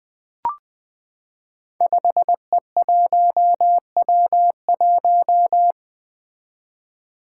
Morse code, heard as a keyed tone, about 20 wpm, 700 Hz. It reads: 5E1W1
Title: Five Challenging Letter-Number Combinations 20wpm